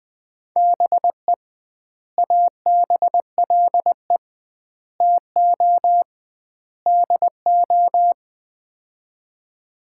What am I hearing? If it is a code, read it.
Morse code, heard as a keyed tone, about 21 wpm, 700 Hz. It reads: BE ABLE TO DO
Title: 2x Word-Spacing - Sets of 4 Words - 20wpm